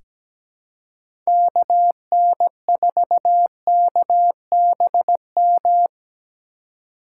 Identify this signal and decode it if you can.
Morse code, heard as a keyed tone, about 17 wpm, 700 Hz. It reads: KN4KBM
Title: USA Call Signs - 4x Character Spacing - 17wpm